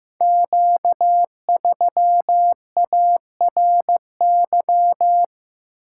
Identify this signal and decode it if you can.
Morse code, heard as a keyed tone, about 15 wpm, 700 Hz. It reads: Q3ARY
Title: Code Groups - 5 Letter:Numbers - 15wpm